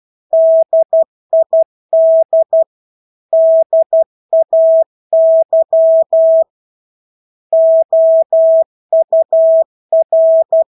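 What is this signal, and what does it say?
Morse code, heard as a keyed tone, about 12 wpm, 650 Hz. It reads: DID DAY OUR
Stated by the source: Morse practice of English words at 12 wpm.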